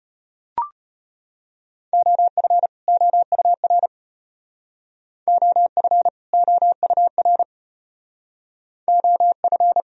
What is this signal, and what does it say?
Morse code, heard as a keyed tone, about 32 wpm, 700 Hz. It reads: OF OUR OF OUR OF